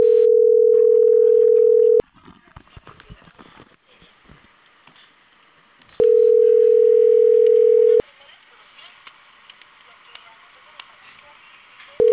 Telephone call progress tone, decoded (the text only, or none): ringback tone